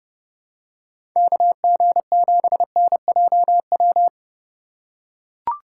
Morse code, heard as a keyed tone, about 31 wpm, 700 Hz. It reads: KG7NJW
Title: USA Call Signs - 6 Characters - 4x Character Spacing - 30wpm